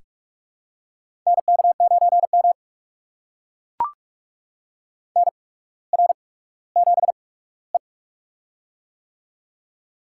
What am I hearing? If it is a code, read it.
Morse code, heard as a keyed tone, about 44 wpm, spaced as for 19 wpm, 700 Hz. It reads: NK9M NR7E
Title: USA Call Signs - 8x Character Spacing - 45wpm